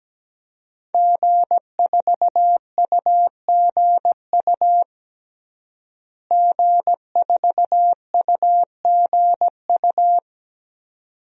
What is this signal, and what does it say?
Morse code, heard as a keyed tone, about 17 wpm, 700 Hz. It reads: G4UGU G4UGU
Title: DXCC-100 Call Signs - 5 Characters - Repeated 2x - 17wpm